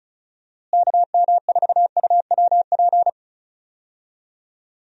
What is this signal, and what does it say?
Morse code, heard as a keyed tone, about 36 wpm, 700 Hz. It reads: KM4UWP
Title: USA Call Signs - 6 Characters - Repeated 3x - 35wpm